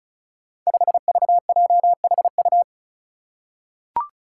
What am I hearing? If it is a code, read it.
Morse code, heard as a keyed tone, about 36 wpm, 700 Hz. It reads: 5VJHU
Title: Five Challenging Letter-Number Combinations 35wpm